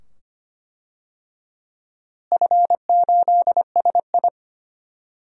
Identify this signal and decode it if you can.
Morse code, heard as a keyed tone, about 25 wpm, 700 Hz. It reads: F8SI
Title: DXCC-100 Call Signs - 4 Characters - Repeated 2x - 25wpm